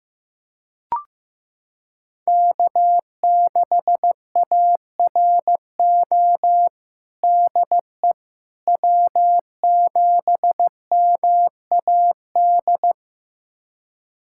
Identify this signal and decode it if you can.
Morse code, heard as a keyed tone, about 15 wpm, 700 Hz. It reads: K6ARO DE W7MAD
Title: USA Callsign DE Callsign - 5 Characters - 15wpm